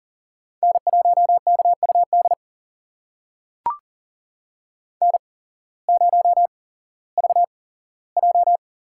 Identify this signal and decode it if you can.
Morse code, heard as a keyed tone, about 40 wpm, spaced as for 19 wpm, 700 Hz. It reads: N1KUD N0VJ
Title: USA Call Signs - 8x Character Spacing - 40wpm